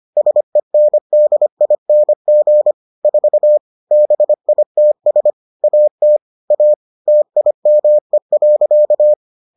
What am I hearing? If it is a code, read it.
Morse code, heard as a keyed tone, about 25 wpm, 600 Hz. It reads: SENDING 4 BITS AT A TIME.